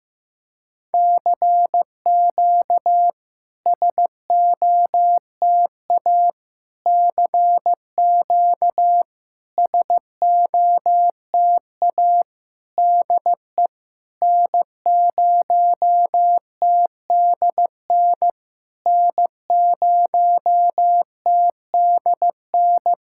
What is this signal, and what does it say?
Morse code, heard as a keyed tone, about 15 wpm, 700 Hz. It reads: CQ SOTA CQ SOTA DE N0TDN N0TDN
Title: SOTA QSO Practice 15wpm